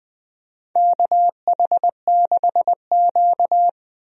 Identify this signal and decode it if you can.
Morse code, heard as a keyed tone, about 21 wpm, 700 Hz. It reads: KH6Q
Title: DXCC-100 Call Signs - 4 Characters - Repeated 3x - 20wpm